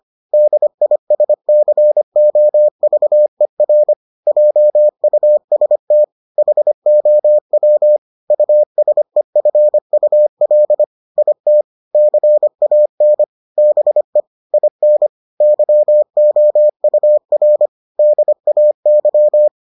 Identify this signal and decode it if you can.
Morse code, heard as a keyed tone, about 25 wpm, 600 Hz. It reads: DISCOVER JUST HOW USEFUL IT CAN BE IN YOUR DAY